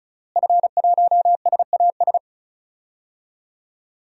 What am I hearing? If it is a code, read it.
Morse code, heard as a keyed tone, about 34 wpm, 700 Hz. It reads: F1SAS
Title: DXCC-100 Call Signs - 5 Characters - Repeated 3x - 35wpm